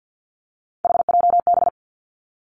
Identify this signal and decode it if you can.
Morse code, heard as a keyed tone, about 49 wpm, 700 Hz. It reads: HJB
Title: Three Challenging Letter-Number Combinations 50wpm